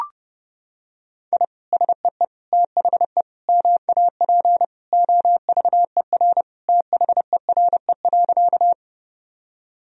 Morse code, heard as a keyed tone, about 30 wpm, 700 Hz. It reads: I SEE THE MAP OVER THERE.
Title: Sentences from Top 1000 Words - Review Part 2 - 30wpm